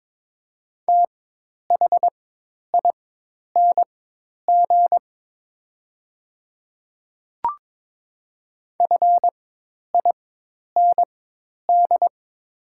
Morse code, heard as a keyed tone, about 22 wpm, spaced as for 10 wpm, 700 Hz. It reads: THING FIND
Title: 4x Character Spacing - Top 100 Words - 22wpm